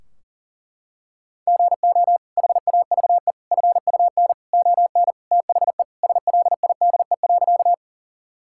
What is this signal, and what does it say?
Morse code, heard as a keyed tone, about 40 wpm, 700 Hz. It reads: GO HAVE FUN ON THE SLIDE.
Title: Sentences from Top 1600 Words 40wpm